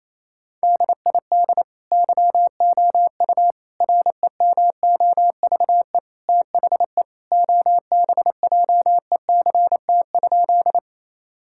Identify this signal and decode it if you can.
Morse code, heard as a keyed tone, about 27 wpm, 700 Hz. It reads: DID YOU REMOVE THE OBJECT?
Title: Sentences from Top 1000 Words - Review Part 5 - 28wpm